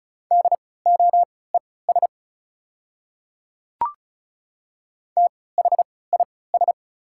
Morse code, heard as a keyed tone, about 34 wpm, spaced as for 20 wpm, 700 Hz. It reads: DOES THIS
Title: 3x Character Spacing - Top 500 Words - 35wpm